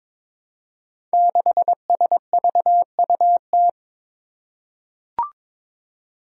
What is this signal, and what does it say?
Morse code, heard as a keyed tone, about 22 wpm, 700 Hz. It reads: 6SVUT
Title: Five Challenging Letter-Number Combinations 22wpm